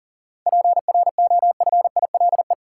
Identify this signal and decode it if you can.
Morse code, heard as a keyed tone, about 40 wpm, 700 Hz. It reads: PROFILE
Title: Seven Letter Words - 40wpm